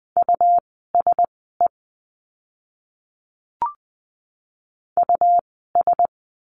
Morse code, heard as a keyed tone, about 20 wpm, spaced as for 15 wpm, 700 Hz. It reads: USE US
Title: 115 Lesson 19f - Letter w - Words 20wpm